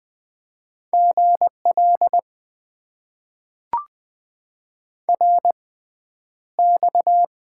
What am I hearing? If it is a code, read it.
Morse code, heard as a keyed tone, about 20 wpm, spaced as for 10 wpm, 700 Hz. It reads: GL RX
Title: Top 90 QSO Elements - 6x Character Spacing - 20wpm